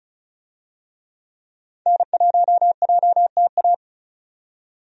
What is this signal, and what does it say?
Morse code, heard as a keyed tone, about 36 wpm, 700 Hz. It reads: N1JTA
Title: USA Call Signs - 5 Characters - 35wpm